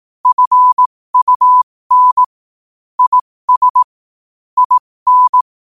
Morse code, heard as a keyed tone, about 18 wpm, spaced as for 15 wpm, 1000 Hz. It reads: FUN IS IN
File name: qotdfiles / 15wpm